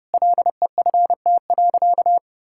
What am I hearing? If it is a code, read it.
Morse code, heard as a keyed tone, about 30 wpm, 700 Hz. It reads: LEFT.
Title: Sentences from Top 1600 Words 30wpm